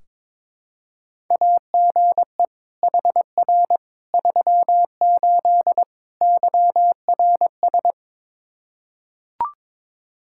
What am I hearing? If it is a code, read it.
Morse code, heard as a keyed tone, about 22 wpm, 700 Hz. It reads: AGE HR 38 YRS
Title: Practice QSO - Exchange 3 - More Info - 22wpm